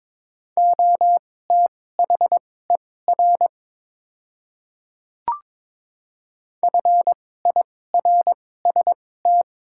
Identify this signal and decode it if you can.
Morse code, heard as a keyed tone, about 22 wpm, spaced as for 16 wpm, 700 Hz. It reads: OTHER FIRST
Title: 2x Character Spacing - Top 100 Words - 22wpm